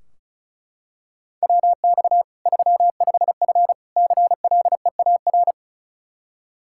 Morse code, heard as a keyed tone, about 36 wpm, 700 Hz. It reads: WX 35F CLEAR
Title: K1USN SST QSO Practice 35wpm